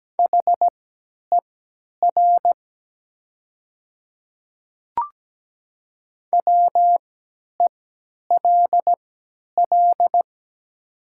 Morse code, heard as a keyed tone, about 17 wpm, spaced as for 10 wpm, 700 Hz. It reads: HER WELL
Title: 3x Character Spacing - Top 200 Words - 17wpm